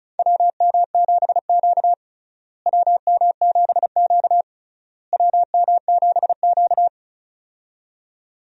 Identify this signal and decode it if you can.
Morse code, heard as a keyed tone, about 36 wpm, 700 Hz. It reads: WM7Q WM7Q WM7Q
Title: USA Call Signs - 4 Characters - Repeated 3x - 35wpm